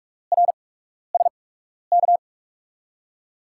Morse code, heard as a keyed tone, about 44 wpm, spaced as for 12 wpm, 700 Hz. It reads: RSK
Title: USA Call Signs - 5 Characters - 8x Character Spacing - 45wpm